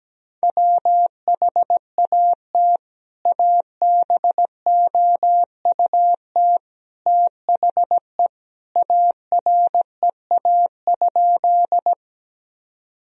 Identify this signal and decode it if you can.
Morse code, heard as a keyed tone, about 17 wpm, 700 Hz. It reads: WHAT ABOUT THE AREA?
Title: Sentences from Top 1000 Words - Review Part 2 - 17wpm